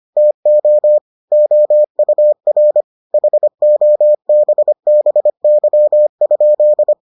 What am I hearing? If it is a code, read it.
Morse code, heard as a keyed tone, about 25 wpm, 600 Hz. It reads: TO OUR HOBBY?